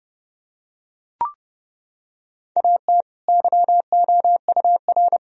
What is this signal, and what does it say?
Morse code, heard as a keyed tone, about 30 wpm, 700 Hz. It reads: AT YOUR
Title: Sentences from Top 1000 Words - Review Part 4 - 30wpm